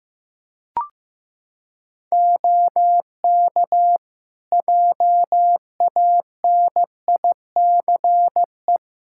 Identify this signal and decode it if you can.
Morse code, heard as a keyed tone, about 15 wpm, 700 Hz. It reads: OK JANICE
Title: Practice QSO - Exchange 3 - More Info - 15wpm